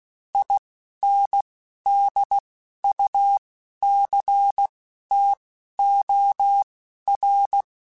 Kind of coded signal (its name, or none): Morse code